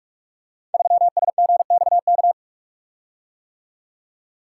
Morse code, heard as a keyed tone, about 44 wpm, 690 Hz. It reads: 3SGXK
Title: Code Groups - 5 Letter:Numbers - 45wpm